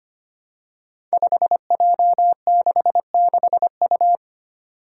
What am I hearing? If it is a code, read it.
Morse code, heard as a keyed tone, about 25 wpm, 700 Hz. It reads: 5J66U
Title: Five Challenging Letter-Number Combinations 25wpm